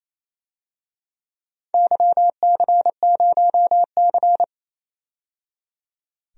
Morse code, heard as a keyed tone, about 28 wpm, 700 Hz. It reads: YC0C
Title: DXCC-100 Call Signs - 4 Characters - 28wpm